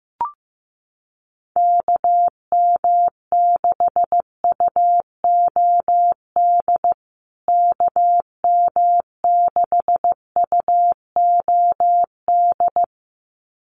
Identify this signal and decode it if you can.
Morse code, heard as a keyed tone, about 15 wpm, 700 Hz. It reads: KM6UOD KM6UOD
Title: USA Call Signs Repeated 2x 15wpm